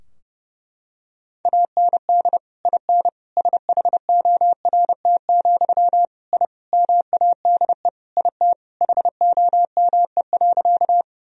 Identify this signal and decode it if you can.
Morse code, heard as a keyed tone, about 32 wpm, 700 Hz. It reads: AND IN SHORT, I MADE IT HOME.
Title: Sentences from Top 1000 Words - Review Part 2 - 30wpm